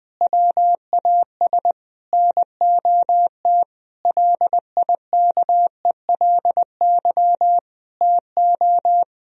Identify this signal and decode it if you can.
Morse code, heard as a keyed tone, about 20 wpm, 700 Hz. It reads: WAS NOT LIKELY TO
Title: Sentences from Top 1000 Words - Review Part 4 - 20wpm